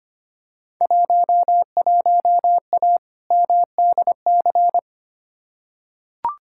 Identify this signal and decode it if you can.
Morse code, heard as a keyed tone, about 25 wpm, 700 Hz. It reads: 11A MDC
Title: Field Day Exchanges Encore - 25wpm